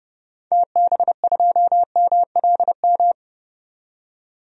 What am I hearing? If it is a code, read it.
Morse code, heard as a keyed tone, about 30 wpm, 700 Hz. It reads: TB2MLM
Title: DXCC-100 Call Signs - 6 Characters - 30wpm